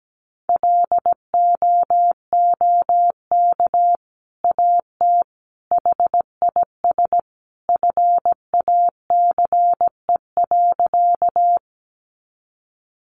Morse code, heard as a keyed tone, about 17 wpm, 700 Hz. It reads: LOOK AT HIS FACE.
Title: Sentences from Top 300 Words 17wpm